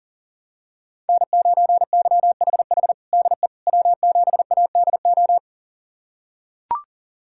Morse code, heard as a keyed tone, about 40 wpm, 700 Hz. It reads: N9YHH DE W7ADO
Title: USA Callsign DE Callsign - 5 Characters - 40wpm